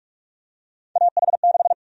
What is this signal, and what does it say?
Morse code, heard as a keyed tone, about 44 wpm, 700 Hz. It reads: AH6
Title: USA Call Sign Prefixes - Repeated 2x - 45wpm